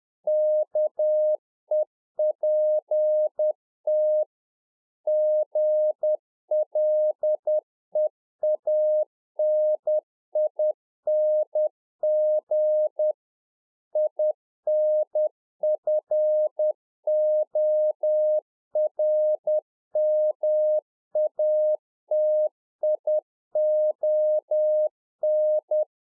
Morse code, heard as a keyed tone, about 10 wpm, 610 Hz. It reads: KEPT GLEANING INFORMATION